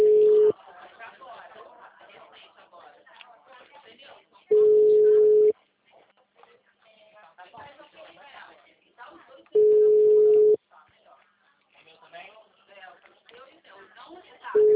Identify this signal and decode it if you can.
Ringback tone (Europe, 425 Hz), with speech.